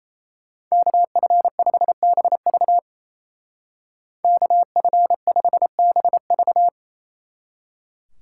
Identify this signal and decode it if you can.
Morse code, heard as a keyed tone, about 34 wpm, 700 Hz. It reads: KF5BV KF5BV
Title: Speed Racing - USA 2x2 Call Signs 28wpm